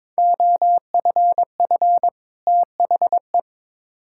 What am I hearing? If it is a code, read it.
Morse code, heard as a keyed tone, about 21 wpm, 700 Hz. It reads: OFF THE
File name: Build A Phrase - 2-Word Phrases - 22wpm